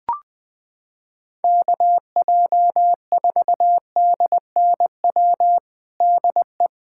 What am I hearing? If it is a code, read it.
Morse code, heard as a keyed tone, about 21 wpm, 700 Hz. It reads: KJ4DNW DE
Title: USA Callsign DE Callsign - 6 Characters - 20wpm